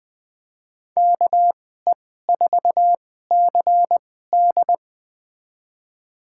Morse code, heard as a keyed tone, about 21 wpm, spaced as for 15 wpm, 700 Hz. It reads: KE4CD
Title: USA Call Signs - 2x Character Spacing - 20wpm